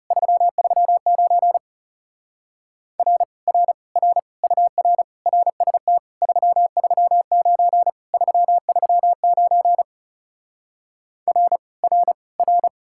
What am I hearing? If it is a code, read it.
Morse code, heard as a keyed tone, about 40 wpm, 700 Hz. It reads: 339 R R R UR RST 339 339 R R R